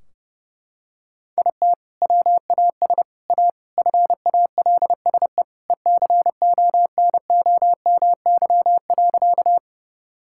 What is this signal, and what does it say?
Morse code, heard as a keyed tone, about 30 wpm, 700 Hz. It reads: IT WAS A FALSE ECONOMY.
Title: Sentences from Top 1600 Words 30wpm